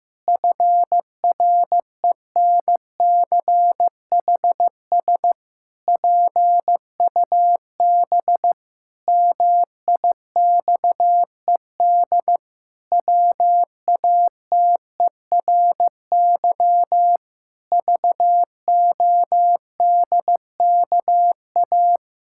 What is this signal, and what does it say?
Morse code, heard as a keyed tone, about 15 wpm, 700 Hz. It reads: FRENCHS PUB MIXED WATERY VODKA
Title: Pangram Challenge v2 - 15wpm